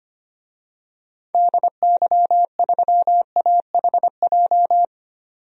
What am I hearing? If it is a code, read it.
Morse code, heard as a keyed tone, about 25 wpm, 700 Hz. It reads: DY3AHJ
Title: DXCC-100 Call Signs - 6 Characters - 25wpm